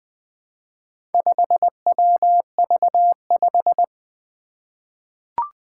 Morse code, heard as a keyed tone, about 20 wpm, 700 Hz. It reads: 5WV5
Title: Four Letter-Number Combinations - 20wpm